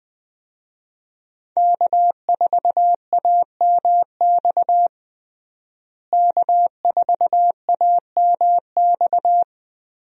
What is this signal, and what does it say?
Morse code, heard as a keyed tone, about 20 wpm, 700 Hz. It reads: K4AMX K4AMX
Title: USA Call Signs - 5 Characters - Repeated 2x - 20wpm